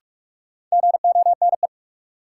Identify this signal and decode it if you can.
Morse code, heard as a keyed tone, about 46 wpm, 700 Hz. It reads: GONE
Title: Sentences from Top 1200 Words - New Word Drill Alt - 45wpm